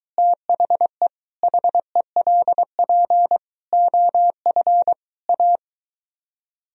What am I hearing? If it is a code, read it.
Morse code, heard as a keyed tone, about 23 wpm, 700 Hz. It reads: THE HELP OF A